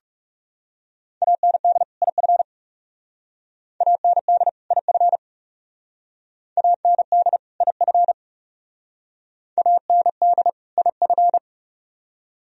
Morse code, heard as a keyed tone, about 40 wpm, 700 Hz. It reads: AND IF AND IF AND IF AND IF